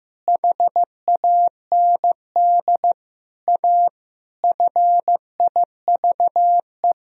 Morse code, heard as a keyed tone, about 16 wpm, 700 Hz. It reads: HAND A FIVE
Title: Sentences from Top 1000 Words - Review Part 2 - 15wpm